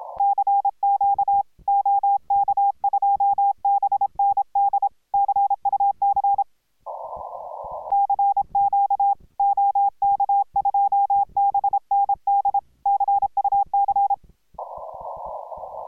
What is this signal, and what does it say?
Morse code, heard as a keyed tone, about 26 wpm, 810 Hz. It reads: CQ OK2BND CUC CQ OK2BND CUC